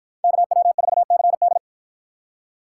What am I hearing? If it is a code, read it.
Morse code, heard as a keyed tone, about 53 wpm, 700 Hz. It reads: KW4CD